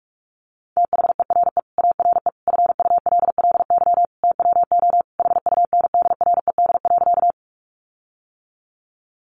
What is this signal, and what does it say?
Morse code, heard as a keyed tone, about 44 wpm, 700 Hz. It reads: THERE ARE FULLY TWO HUNDRED.
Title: Sentences from Top 1300 Words 45wpm